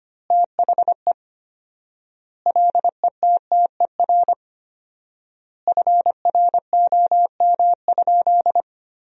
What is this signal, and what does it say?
Morse code, heard as a keyed tone, about 25 wpm, 700 Hz. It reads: THE LETTER FROM?
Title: 4x Word-Spacing - Sentences from Top 300 Words - 25wpm